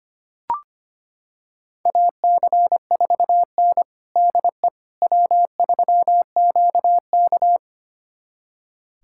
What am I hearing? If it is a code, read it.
Morse code, heard as a keyed tone, about 25 wpm, 700 Hz. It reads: AC4N DE W3QK
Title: USA Callsign DE Callsign - 4 Characters - 25wpm